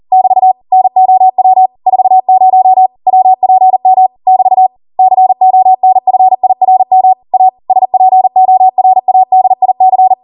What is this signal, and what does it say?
Morse code, heard as a keyed tone, about 40 wpm, 750 Hz. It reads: <BT> NOW 40 WPM <BT> CONFIRM A SPORADIC